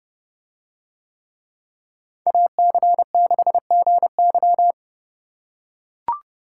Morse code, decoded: AC6GY

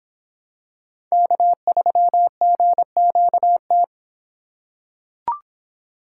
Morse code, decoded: K3GQT

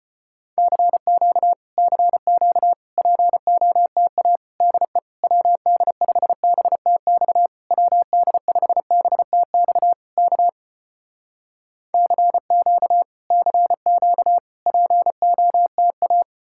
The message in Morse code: CQ CQ POTA DE WD5BTX WD5BTX K CQ CQ POTA

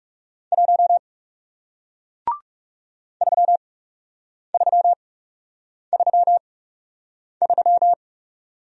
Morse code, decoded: 1 3333

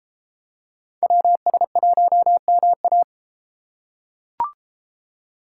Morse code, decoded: WS1MA

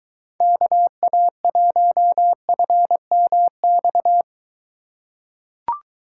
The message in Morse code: KA1FMX

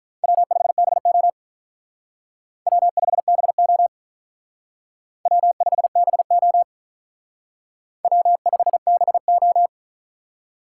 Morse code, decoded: W5BO W5BO W5BO W5BO

